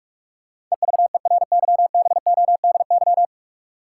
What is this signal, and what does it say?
Morse code, heard as a keyed tone, about 46 wpm, 700 Hz. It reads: EVERYBODY
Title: Sentences from Top 1400 Words - New Word Drill Alt - 45wpm